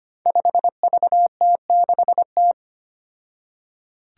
Morse code, heard as a keyed tone, about 25 wpm, 700 Hz. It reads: 5VT6T
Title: Five Challenging Letter-Number Combinations 25wpm